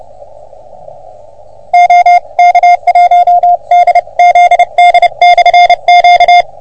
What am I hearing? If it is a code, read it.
Morse code, heard as a keyed tone, about 28 wpm, 710 Hz. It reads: OK1DZD/Q
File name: OM3PA PILUP CUC 388